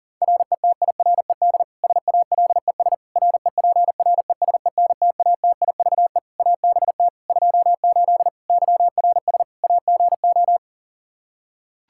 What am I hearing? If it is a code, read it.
Morse code, heard as a keyed tone, about 37 wpm, 700 Hz. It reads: RETIRED SALES REPRESENTATIVE ABT 28 YRS AGO
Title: Practice QSO - Exchange 3 - More Info - 40wpm